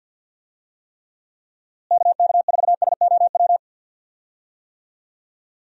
Morse code, heard as a keyed tone, about 52 wpm, 700 Hz. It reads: KK4SOW
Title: USA Call Signs - 6 Characters - Repeated 2x - 50wpm